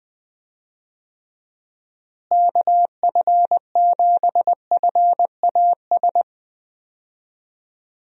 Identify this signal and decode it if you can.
Morse code, heard as a keyed tone, about 20 wpm, 700 Hz. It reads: KF7FAS
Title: USA Call Signs - 6 Characters - Repeated 2x - 20wpm